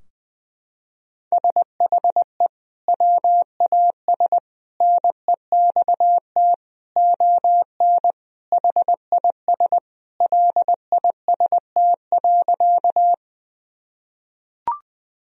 Morse code, heard as a keyed tone, about 20 wpm, 700 Hz. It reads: SHE WAS NEXT ON HIS LIST.
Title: Sentences from Top 1000 Words - Review Part 2 - 20wpm